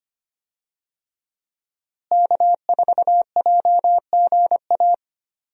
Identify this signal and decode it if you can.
Morse code, heard as a keyed tone, about 25 wpm, 700 Hz. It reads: K4JGA
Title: USA Call Signs - 5 Characters - Repeated 3x - 25wpm